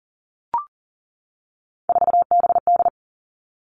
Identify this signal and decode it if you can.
Morse code, heard as a keyed tone, about 38 wpm, 690 Hz. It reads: 4BD
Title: Three Challenging Letter-Number Combinations 40wpm